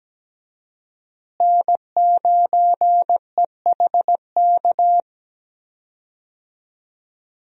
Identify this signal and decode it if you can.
Morse code, heard as a keyed tone, about 17 wpm, 700 Hz. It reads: N9EHK
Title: USA Call Signs - 5 Characters - Repeated 3x - 17wpm